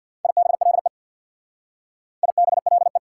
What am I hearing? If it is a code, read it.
Morse code, heard as a keyed tone, about 66 wpm, 700 Hz. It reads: IBLE IBLE